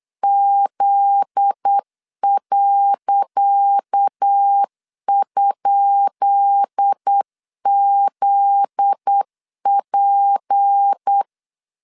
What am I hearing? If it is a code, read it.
Morse code, heard as a keyed tone, about 8 wpm, 800 Hz. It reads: Z.?ZP
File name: LEZIONI / 10 SIGNS / 10D COMA 8wpm 3749